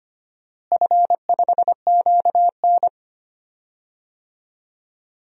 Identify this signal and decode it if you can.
Morse code, heard as a keyed tone, about 25 wpm, 700 Hz. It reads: F5QN
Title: DXCC-100 Call Signs - 4 Characters - Repeated 3x - 25wpm